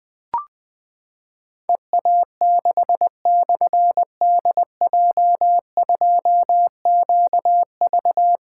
Morse code, heard as a keyed tone, about 20 wpm, 700 Hz. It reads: EA6/DJ2QV
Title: Difficult Call Signs 20wpm